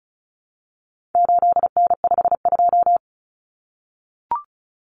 Morse code, heard as a keyed tone, about 34 wpm, 700 Hz. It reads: 8N52
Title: Common Cut Numbers - 4-Digit 35wpm